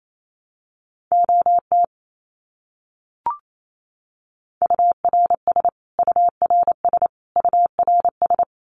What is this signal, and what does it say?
Morse code, heard as a keyed tone, about 28 wpm, 700 Hz. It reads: OT URS URS URS
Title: Abbreviations Repeated 3x - 28wpm